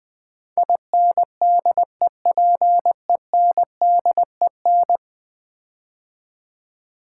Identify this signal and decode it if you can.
Morse code, heard as a keyed tone, about 20 wpm, 700 Hz. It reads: INDEPENDEN
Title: Build A Word - 5 Letters At A Time - 12-Letter Words - 20wpm